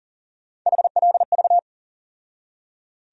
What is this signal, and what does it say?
Morse code, heard as a keyed tone, about 38 wpm, 700 Hz. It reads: HLV
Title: USA Call Sign Postfixes - 3 Letters 40wpm